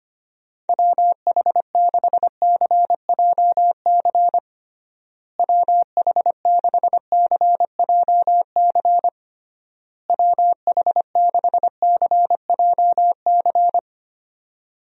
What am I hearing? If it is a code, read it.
Morse code, heard as a keyed tone, about 25 wpm, 700 Hz. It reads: WH6CJC WH6CJC WH6CJC
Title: USA Call Signs - 6 Characters - Repeated 3x - 25wpm